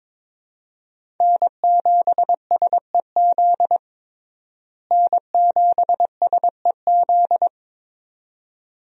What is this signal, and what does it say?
Morse code, heard as a keyed tone, about 22 wpm, 700 Hz. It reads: N7SEZ N7SEZ
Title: USA Call Signs - 5 Characters - Repeated 2x - 22wpm